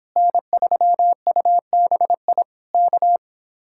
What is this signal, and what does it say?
Morse code, heard as a keyed tone, about 26 wpm, 700 Hz. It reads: N3UBI K